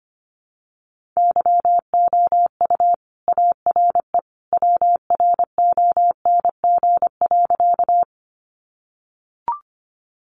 Morse code, heard as a keyed tone, about 25 wpm, 700 Hz. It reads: YOU ARE WRONG.
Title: Sentences from Top 900 Words 25wpm